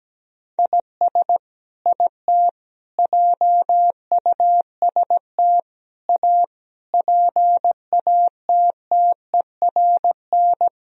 Morse code, heard as a keyed tone, about 17 wpm, 700 Hz. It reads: IS IT JUST A PATTERN
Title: Sentences from Top 1000 Words - Review Part 2 - 17wpm